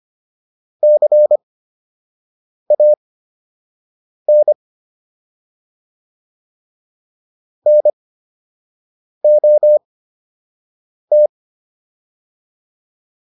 Morse code, heard as a keyed tone, about 25 wpm, spaced as for 6 wpm, 600 Hz. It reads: CAN NOT